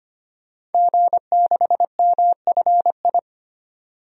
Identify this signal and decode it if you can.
Morse code, heard as a keyed tone, about 25 wpm, 700 Hz. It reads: G6MFI
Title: DXCC-100 Call Signs - 5 Characters - 25wpm